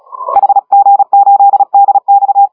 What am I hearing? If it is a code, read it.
Morse code, heard as a keyed tone, about 37 wpm, 810 Hz. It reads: HG8DX
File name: HG1R CW SSB MIX